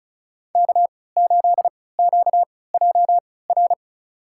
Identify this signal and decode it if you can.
Morse code, heard as a keyed tone, about 34 wpm, spaced as for 20 wpm, 700 Hz. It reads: K8QJR
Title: USA Call Signs - 5 Characters - 3x Character Spacing - 35wpm